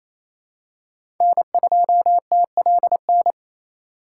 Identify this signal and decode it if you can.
Morse code, heard as a keyed tone, about 28 wpm, 700 Hz. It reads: N2TLN
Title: USA Call Signs - 2x Character Spacing - 28wpm